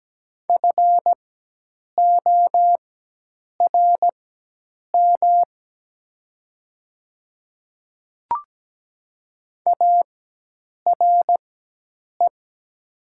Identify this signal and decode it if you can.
Morse code, heard as a keyed tone, about 17 wpm, spaced as for 8 wpm, 700 Hz. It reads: FORM ARE
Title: 4x Character Spacing - Top 200 Words - 17wpm